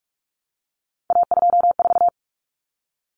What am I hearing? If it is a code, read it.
Morse code, heard as a keyed tone, about 44 wpm, 700 Hz. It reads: A24
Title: Common Cut Numbers - 3-Digit 45wpm